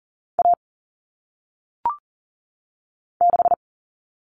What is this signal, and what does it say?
Morse code, heard as a keyed tone, about 40 wpm, 700 Hz. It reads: A 6
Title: All Cut Numbers - 1-Digit 40wpm